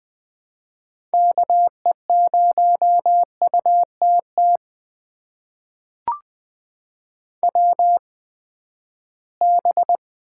Morse code, heard as a keyed tone, about 20 wpm, spaced as for 14 wpm, 700 Hz. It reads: KE0UTT WB
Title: USA Call Signs - 6 Characters - 8x Character Spacing - 20wpm